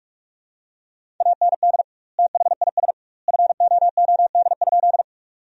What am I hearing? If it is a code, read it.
Morse code, heard as a keyed tone, about 47 wpm, 700 Hz. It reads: AND THIS FOOD?
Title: 2x Word-Spacing - Sentences from Top 300 Words - 45wpm